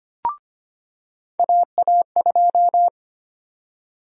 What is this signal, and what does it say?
Morse code, heard as a keyed tone, about 25 wpm, 700 Hz. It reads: AA2
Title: USA Call Sign Prefixes - 25wpm